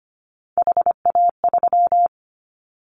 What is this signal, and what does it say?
Morse code, heard as a keyed tone, about 25 wpm, 690 Hz. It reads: HA3
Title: Callsign Prefixes for Top 50 DXCC - 25wpm